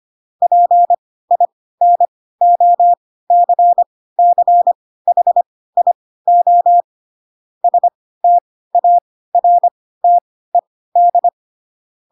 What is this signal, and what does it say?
Morse code, heard as a keyed tone, about 25 wpm, spaced as for 16 wpm, 700 Hz. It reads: PINOCCHIO STARTED
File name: Pinocchio / Pinocchio ch3 10 25 ik1hge